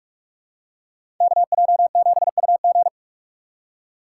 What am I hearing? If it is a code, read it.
Morse code, heard as a keyed tone, about 47 wpm, 700 Hz. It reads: KJ7UG
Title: USA Call Signs - 2x Character Spacing - 45wpm